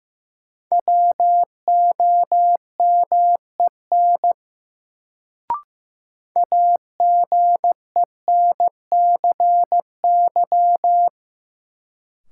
Morse code, heard as a keyed tone, about 15 wpm, 700 Hz. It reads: WOMEN AGENCY
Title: Sentences from Top 1300 Words - New Word Drill - 15wpm